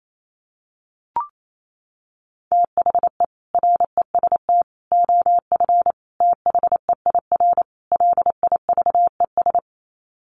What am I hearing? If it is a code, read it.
Morse code, heard as a keyed tone, about 28 wpm, 700 Hz. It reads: THE REST OF THEIR LIVES
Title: Sets of 5 Words Encore 28wpm